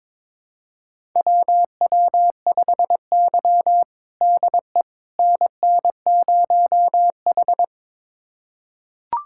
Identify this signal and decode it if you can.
Morse code, heard as a keyed tone, about 22 wpm, 700 Hz. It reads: WW5Y DE NN0H
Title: USA Callsign DE Callsign - 4 Characters - 22wpm